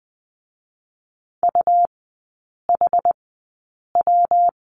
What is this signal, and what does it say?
Morse code, heard as a keyed tone, about 20 wpm, spaced as for 8 wpm, 700 Hz. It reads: UHW
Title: 127 Lesson 21e - Number 6 - Three Characters 20wpm